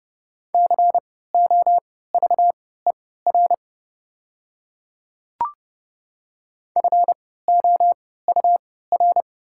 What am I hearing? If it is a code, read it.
Morse code, heard as a keyed tone, about 30 wpm, spaced as for 17 wpm, 690 Hz. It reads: COVER FOUR
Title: 3x Character Spacing - Top 300 Words - 30wpm